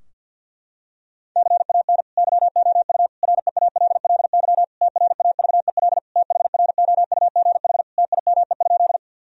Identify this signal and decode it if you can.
Morse code, heard as a keyed tone, about 51 wpm, 700 Hz. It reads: CAN YOU REALLY TRAVEL THROUGH TIME?
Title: Sentences from Top 1000 Words - Review Part 2 - 50wpm